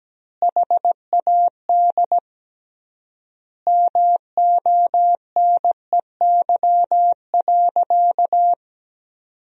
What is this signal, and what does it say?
Morse code, heard as a keyed tone, about 17 wpm, 700 Hz. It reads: HAD MONEY.
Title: 3x Word-Spacing - Sentences from Top 500 Words - 17wpm